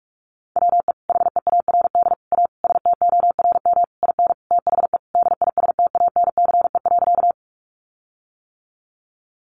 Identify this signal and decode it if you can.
Morse code, heard as a keyed tone, about 44 wpm, 700 Hz. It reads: WE HEARD A STORM IN THE DISTANCE.
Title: Sentences from Top 1500 Words 45wpm